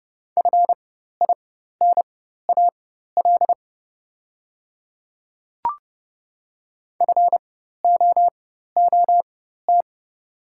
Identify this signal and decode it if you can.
Morse code, heard as a keyed tone, about 30 wpm, spaced as for 14 wpm, 690 Hz. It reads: FINAL FOOT
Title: 4x Character Spacing - Top 500 Words - 30wpm